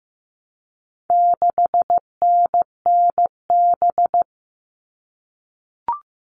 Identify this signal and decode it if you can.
Morse code, heard as a keyed tone, about 15 wpm, 700 Hz. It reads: 6NNB
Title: All Cut Numbers - 4-Digit 15wpm